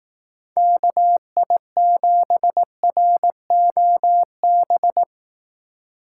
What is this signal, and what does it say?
Morse code, heard as a keyed tone, about 18 wpm, 700 Hz. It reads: KI7ROB
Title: Speed Racing - USA 2x3 Call Signs 15wpm